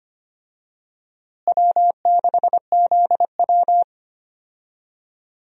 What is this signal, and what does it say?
Morse code, heard as a keyed tone, about 25 wpm, 700 Hz. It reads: W6ZW
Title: USA Call Signs - 4 Characters - Repeated 2x - 25wpm